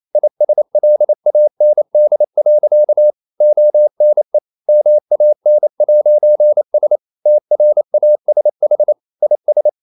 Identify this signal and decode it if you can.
Morse code, heard as a keyed tone, about 28 wpm, 600 Hz. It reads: ISLAND. ONE MAN'S TRASH IS
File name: CW / Practice / EnglishSayings28